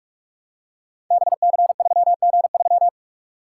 Morse code, heard as a keyed tone, about 46 wpm, 700 Hz. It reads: BC3G3
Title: Code Groups - 5 Letter:Numbers - 45wpm